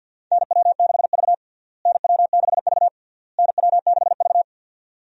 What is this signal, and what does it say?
Morse code, heard as a keyed tone, about 46 wpm, 700 Hz. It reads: NW6V NW6V NW6V
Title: USA Call Signs - 4 Characters - Repeated 3x - 50wpm